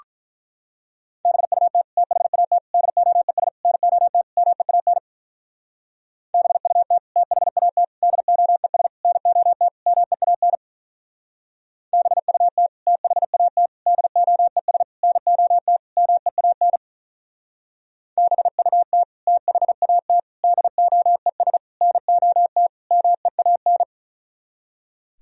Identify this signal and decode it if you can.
Morse code, heard as a keyed tone, about 43 wpm, 700 Hz. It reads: BUT THAT DOES NOT MEAN BUT THAT DOES NOT MEAN BUT THAT DOES NOT MEAN BUT THAT DOES NOT MEAN
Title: Speed-Racing - Sets of 5 Words 35wpm